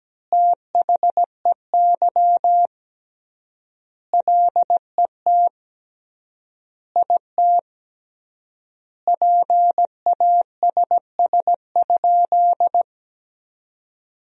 Morse code, heard as a keyed tone, about 17 wpm, 700 Hz. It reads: THEY LET IT PASS?
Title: 3x Word-Spacing - Sentences from Top 400 Words - 17wpm